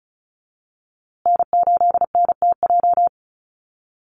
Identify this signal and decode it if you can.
Morse code, heard as a keyed tone, about 34 wpm, 700 Hz. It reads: N8NTJ
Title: USA Call Signs Repeated 3x 35wpm